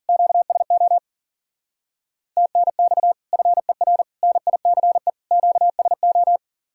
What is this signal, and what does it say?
Morse code, heard as a keyed tone, about 47 wpm, 700 Hz. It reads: QSO TNX FER NICE QSO
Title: Speed-Racing Common QSO Phrases Encore - 40wpm